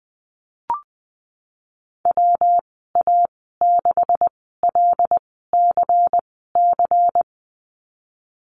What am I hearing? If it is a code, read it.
Morse code, heard as a keyed tone, about 20 wpm, spaced as for 15 wpm, 700 Hz. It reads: WA6LCC
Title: 179 Lesson 29g - Slash - Call Signs 20wpm